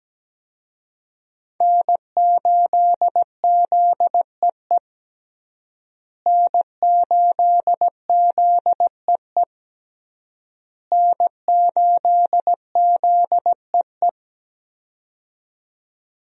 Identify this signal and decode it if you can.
Morse code, heard as a keyed tone, about 17 wpm, 700 Hz. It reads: N8ZEE N8ZEE N8ZEE
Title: USA Call Signs - 5 Characters - Repeated 3x - 17wpm